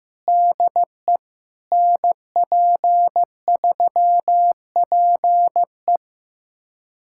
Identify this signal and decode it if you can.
Morse code, heard as a keyed tone, about 16 wpm, 700 Hz. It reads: DE NP3PE
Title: USA Callsign DE Callsign - 5 Characters - 15wpm